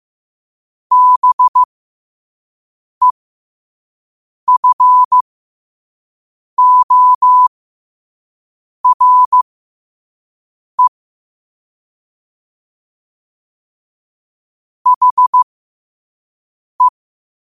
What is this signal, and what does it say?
Morse code, heard as a keyed tone, about 15 wpm, spaced as for 5 wpm, 1000 Hz. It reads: BEFORE HE